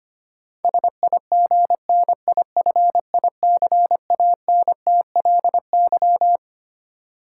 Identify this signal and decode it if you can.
Morse code, heard as a keyed tone, about 25 wpm, 700 Hz. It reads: SIGNIFICANTLY
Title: Sentences from Top 1500 Words - New Word Drill - 25wpm